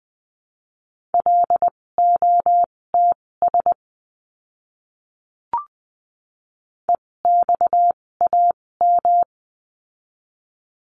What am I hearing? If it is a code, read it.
Morse code, heard as a keyed tone, about 20 wpm, spaced as for 16 wpm, 700 Hz. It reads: LOTS EXAM
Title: 256 Lesson 42a - Words 12-20wpm